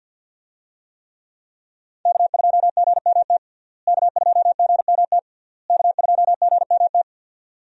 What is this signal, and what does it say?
Morse code, heard as a keyed tone, about 52 wpm, 700 Hz. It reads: K2GMT K2GMT K2GMT
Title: USA Call Signs - 5 Characters - Repeated 3x - 50wpm